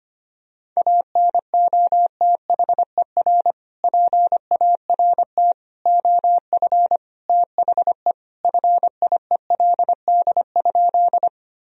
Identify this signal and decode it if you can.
Morse code, heard as a keyed tone, about 25 wpm, 700 Hz. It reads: ANOTHER PART OF THE FIELD?
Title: Sentences from Top 1000 Words - Review Part 2 - 25wpm